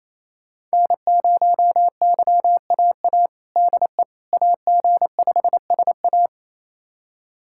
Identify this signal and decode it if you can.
Morse code, heard as a keyed tone, about 28 wpm, 700 Hz. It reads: N0YAA DE AG5SA
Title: USA Callsign DE Callsign 28wpm